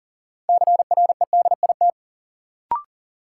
Morse code, heard as a keyed tone, about 38 wpm, 700 Hz. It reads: CREDIT
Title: Sentences from Top 1000 Words - New Word Drill - 40wpm